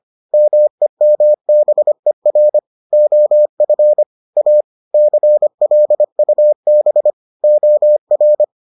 Morse code, heard as a keyed tone, about 25 wpm, 600 Hz. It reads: MEMBER OF A CLUB OR